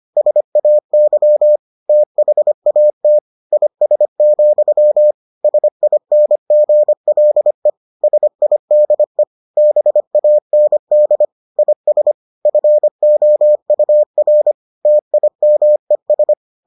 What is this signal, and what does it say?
Morse code, heard as a keyed tone, about 25 wpm, 600 Hz. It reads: SAY THAT IS, SINGLE SIDE BAND IS FOUR TIMES